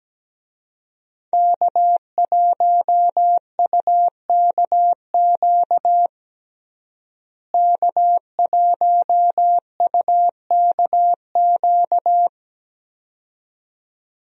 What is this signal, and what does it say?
Morse code, heard as a keyed tone, about 17 wpm, 700 Hz. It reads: K1UKQ K1UKQ
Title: USA Call Signs - 5 Characters - Repeated 2x - 17wpm